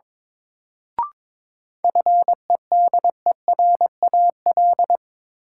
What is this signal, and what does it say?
Morse code, heard as a keyed tone, about 22 wpm, 700 Hz. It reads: FEDERAL
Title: Sets of 3 Words - New Word Drill - 22wpm